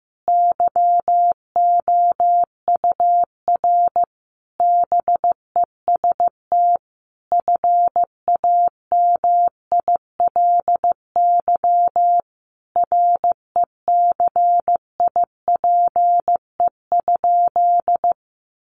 Morse code, heard as a keyed tone, about 15 wpm, 700 Hz. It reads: YOUR BEST FAMILY RECIPE?
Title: Sentences from Top 1200 Words 15wpm